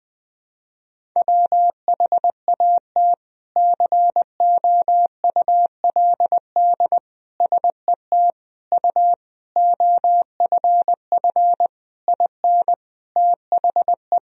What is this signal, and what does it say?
Morse code, heard as a keyed tone, about 21 wpm, 700 Hz. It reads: WHAT COULD SET U OFF IN THE
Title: Sentences from Top 200 Words and Abbreviations 20wpm